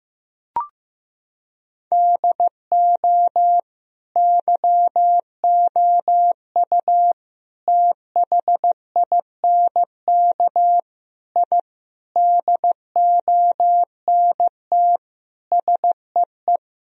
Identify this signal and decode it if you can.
Morse code, heard as a keyed tone, about 16 wpm, 700 Hz. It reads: DO YOU THINK I DONT SEE
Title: Sentences from Top 1000 Words - Review Part 1 - 15wpm